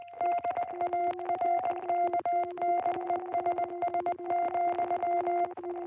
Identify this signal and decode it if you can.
Morse code, heard as a keyed tone, about 20 wpm, 660 Hz. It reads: DARKNESS,